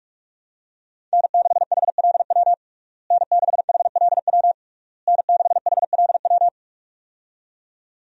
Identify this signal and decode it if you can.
Morse code, heard as a keyed tone, about 44 wpm, 700 Hz. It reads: N6HLW N6HLW N6HLW
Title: USA Call Signs - 5 Characters - Repeated 3x - 45wpm